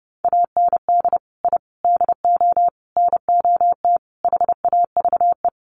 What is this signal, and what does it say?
Morse code, heard as a keyed tone, about 30 wpm, 700 Hz. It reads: AND I DO NOT HAVE
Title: Sets of 5 Words Encore 30wpm